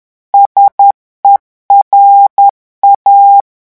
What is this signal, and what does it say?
Morse code, heard as a keyed tone, about 10 wpm, 800 Hz. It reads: SERA
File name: revision9 citations humoristiques vitesse 10 mots minute